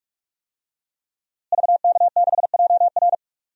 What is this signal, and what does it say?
Morse code, heard as a keyed tone, about 46 wpm, 700 Hz. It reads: VK6JR
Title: DXCC-100 Call Signs - 5 Characters - Repeated 2x - 45wpm